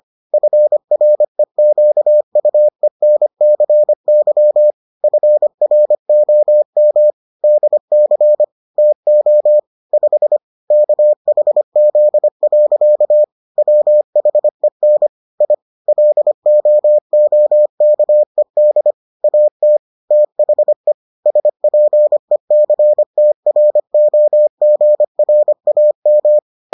Morse code, heard as a keyed tone, about 25 wpm, 600 Hz. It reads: FREQUENCY FROM DC TO 5 KHZ. WHEN I LOOKED AT THE SPECTROGRAM